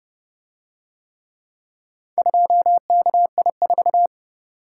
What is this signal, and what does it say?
Morse code, heard as a keyed tone, about 29 wpm, 700 Hz. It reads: 2KI4